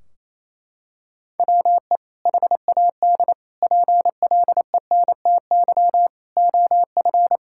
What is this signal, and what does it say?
Morse code, heard as a keyed tone, about 28 wpm, 700 Hz. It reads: WE HAD PLENTY OF
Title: Sentences from Top 1600 Words 28wpm